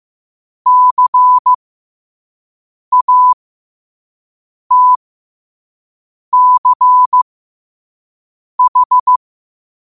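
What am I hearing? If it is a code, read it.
Morse code, heard as a keyed tone, about 15 wpm, spaced as for 5 wpm, 1000 Hz. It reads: CATCH
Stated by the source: 5 wpm Morse code practice recording.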